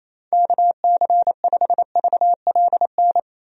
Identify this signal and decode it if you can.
Morse code, heard as a keyed tone, about 27 wpm, 700 Hz. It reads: KC5VLN
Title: USA Call Signs - 6 Characters - 28wpm